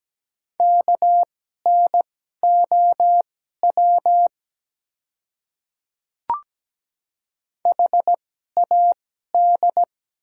Morse code, heard as a keyed tone, about 17 wpm, spaced as for 12 wpm, 700 Hz. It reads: KNOW HAD
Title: 2x Character Spacing - Top 100 Words - 17wpm